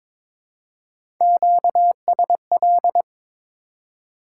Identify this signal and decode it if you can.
Morse code, heard as a keyed tone, about 22 wpm, 700 Hz. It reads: QSL
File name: Top 90 QSO Elements - 4x Character Spacing - 22wpm